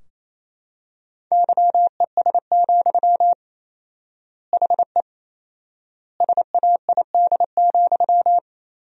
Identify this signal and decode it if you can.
Morse code, heard as a keyed tone, about 28 wpm, 700 Hz. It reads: YES, HE SAID,
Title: 4x Word-Spacing - Sentences from Top 500 Words - 28wpm